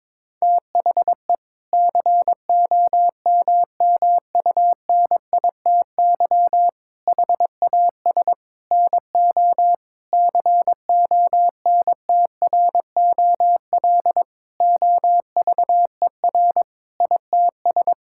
Morse code, heard as a keyed tone, about 22 wpm, 700 Hz. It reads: THE COMMUNITY HAS NO CONTROL OVER ITS
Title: Sentences from Top 1000 Words - Review Part 5 - 22wpm